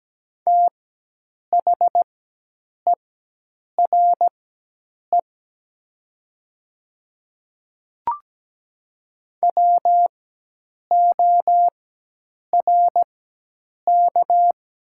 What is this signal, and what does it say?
Morse code, heard as a keyed tone, about 17 wpm, spaced as for 8 wpm, 700 Hz. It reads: THERE WORK
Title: 4x Character Spacing - Top 200 Words - 17wpm